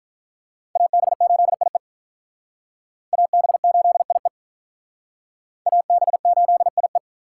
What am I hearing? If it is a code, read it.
Morse code, heard as a keyed tone, about 46 wpm, 700 Hz. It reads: AB8IE AB8IE AB8IE